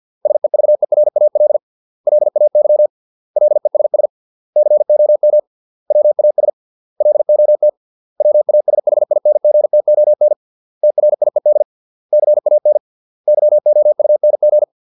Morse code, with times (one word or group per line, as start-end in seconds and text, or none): 0.3-1.6 SEVERAL
2.1-2.9 LAY
3.4-4.1 LESS
4.6-5.4 COM
5.9-6.5 WAS
7.0-7.7 LOT
8.2-10.3 WASHINGTON
10.8-11.6 TRIED
12.1-12.8 CAN
13.3-14.6 YOUNG